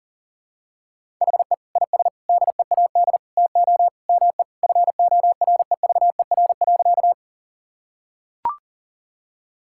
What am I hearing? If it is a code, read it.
Morse code, heard as a keyed tone, about 40 wpm, 700 Hz. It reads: HE IS DEAD TO ME FOREVER.
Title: Sentences from Top 1600 Words 40wpm